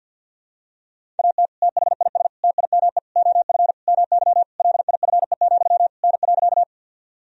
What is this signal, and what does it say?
Morse code, heard as a keyed tone, about 49 wpm, 700 Hz. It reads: AT THIS TIME OF MY LIFE, N.
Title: Sentences from Top 300 Words and Abbreviations 50wpm